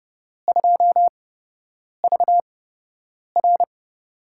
Morse code, heard as a keyed tone, about 31 wpm, spaced as for 8 wpm, 700 Hz. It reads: 2VR
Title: USA Call Signs - 4 Characters - 8x Character Spacing - 30wpm